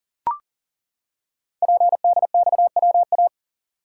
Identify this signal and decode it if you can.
Morse code, heard as a keyed tone, about 40 wpm, 690 Hz. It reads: PDXWA
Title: Code Groups - 5 Letter:Numbers - 40wpm